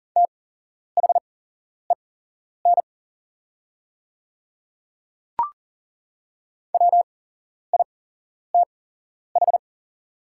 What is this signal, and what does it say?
Morse code, heard as a keyed tone, about 40 wpm, spaced as for 11 wpm, 700 Hz. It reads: THEN WITH
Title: 8x Character Spacing - Top 100 Words - 40wpm